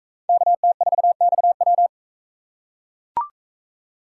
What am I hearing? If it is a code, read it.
Morse code, heard as a keyed tone, about 39 wpm, 700 Hz. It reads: KT4XW